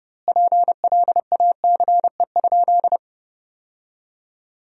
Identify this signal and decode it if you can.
Morse code, heard as a keyed tone, about 30 wpm, 700 Hz. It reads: PLACE?
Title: 2x Word-Spacing - Sentences from Top 200 Words - 30wpm